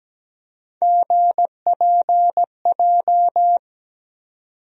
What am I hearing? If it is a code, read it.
Morse code, heard as a keyed tone, about 17 wpm, 700 Hz. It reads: GPJ
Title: USA Call Sign Postfixes - 3 Letters 17wpm